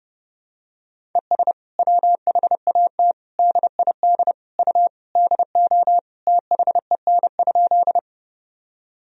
Morse code, heard as a keyed tone, about 30 wpm, 700 Hz. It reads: ES WHAT DID U DO THEN?
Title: Sentences from Top 100 Words and Abbreviations 30wpm